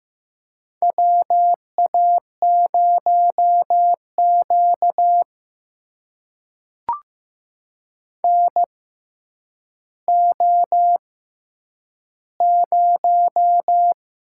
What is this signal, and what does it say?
Morse code, heard as a keyed tone, about 15 wpm, spaced as for 9 wpm, 700 Hz. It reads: WA0Q NO0
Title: USA Call Signs - 4 Characters - 6x Character Spacing - 15wpm